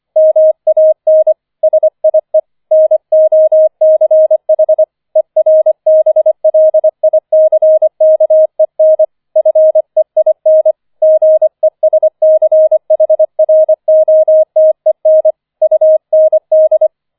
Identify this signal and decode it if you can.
Morse code, heard as a keyed tone, about 24 wpm, 630 Hz. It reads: MAN SIE NOCH ERBLICKEN FEIN GESCHROTEN UND